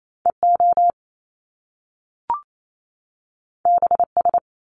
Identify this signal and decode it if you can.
Morse code, heard as a keyed tone, about 28 wpm, 700 Hz. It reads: EO BS
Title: Top Two Letter Combinations in Words 28wpm